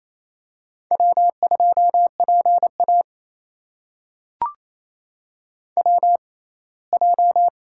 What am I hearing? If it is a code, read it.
Morse code, heard as a keyed tone, about 27 wpm, spaced as for 19 wpm, 700 Hz. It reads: W2PA WJ
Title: USA Call Signs - 4 Characters - 6x Character Spacing - 28wpm